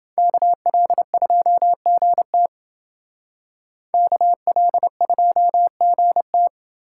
Morse code, heard as a keyed tone, about 28 wpm, 700 Hz. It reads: KL2GT KL2GT